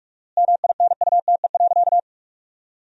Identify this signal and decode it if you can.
Morse code, heard as a keyed tone, about 43 wpm, 700 Hz. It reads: MINUTE.
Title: 4x Word-Spacing - Sentences from Top 500 Words - 45wpm